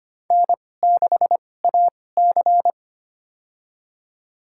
Morse code, heard as a keyed tone, about 25 wpm, spaced as for 18 wpm, 700 Hz. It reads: N6AC
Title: USA Call Signs - 4 Characters - 2x Character Spacing - 25wpm